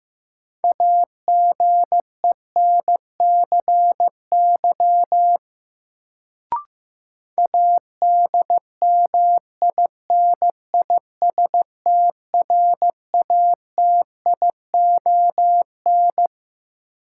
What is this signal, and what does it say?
Morse code, heard as a keyed tone, about 15 wpm, 700 Hz. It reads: AGENCY ADMINISTRATION
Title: Sentences from Top 1300 Words - New Word Drill Alt - 15wpm